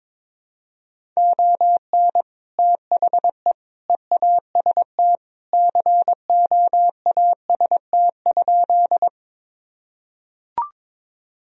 Morse code, heard as a keyed tone, about 22 wpm, 700 Hz. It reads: ON THE EAST COAST?
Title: Sentences from Top 1000 Words - Review Part 4 - 22wpm